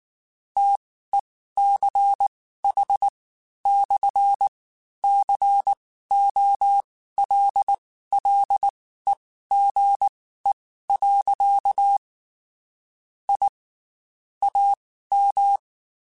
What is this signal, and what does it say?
Morse code, heard as a keyed tone, about 19 wpm, spaced as for 13 wpm, 800 Hz. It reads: TECH/COLLEGE. I AM
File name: QSOs / QSOs-mp3s / WD0BGP 1@18